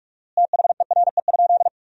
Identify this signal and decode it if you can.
Morse code, heard as a keyed tone, about 46 wpm, 700 Hz. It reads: THERE?
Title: 4x Word-Spacing - Sentences from Top 100 Words - 45wpm